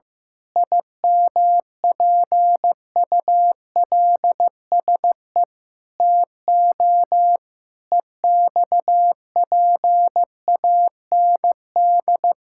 Morse code, heard as a keyed tone, about 15 wpm, 700 Hz. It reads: IMPULSE TO EXPAND
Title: Pangram Challenge v2 - 15wpm